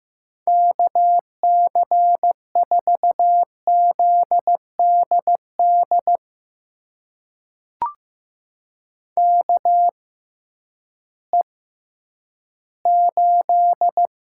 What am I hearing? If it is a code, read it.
Morse code, heard as a keyed tone, about 15 wpm, spaced as for 10 wpm, 700 Hz. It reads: KC4ZDD KE8
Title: USA Call Signs - 6 Characters - 6x Character Spacing - 15wpm